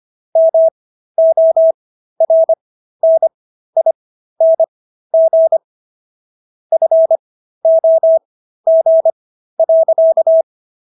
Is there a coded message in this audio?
Yes: Morse code from 0.4 to 10.4 s, heard as a keyed tone, about 25 wpm, spaced as for 13 wpm, 650 Hz. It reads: MORNING FOG.